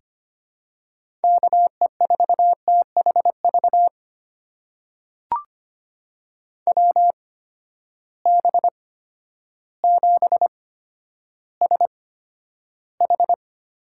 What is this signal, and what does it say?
Morse code, heard as a keyed tone, about 25 wpm, spaced as for 11 wpm, 700 Hz. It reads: KE4THV WB7SH
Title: USA Call Signs - 6 Characters - 8x Character Spacing - 25wpm